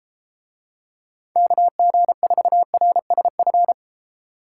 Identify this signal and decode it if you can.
Morse code, heard as a keyed tone, about 29 wpm, 700 Hz. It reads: KG4RSF